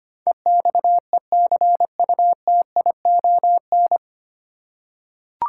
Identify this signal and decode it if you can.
Morse code, heard as a keyed tone, about 25 wpm, 690 Hz. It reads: EXECUTION
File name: practice / 25wpm / 9 Letter Words - First 3 Letters Given - 25wpm